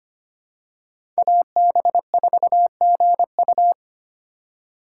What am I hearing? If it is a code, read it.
Morse code, heard as a keyed tone, about 25 wpm, 700 Hz. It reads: AB4GU
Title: USA Call Signs - 4x Character Spacing - 25wpm